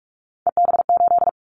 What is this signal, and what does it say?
Morse code, heard as a keyed tone, about 46 wpm, 690 Hz. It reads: EB8
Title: Callsign Prefixes for Top 50 DXCC - 45wpm